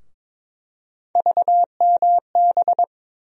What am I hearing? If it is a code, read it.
Morse code, heard as a keyed tone, about 22 wpm, 700 Hz. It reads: VMB
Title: USA Call Sign Postfixes - 3 Letters 22wpm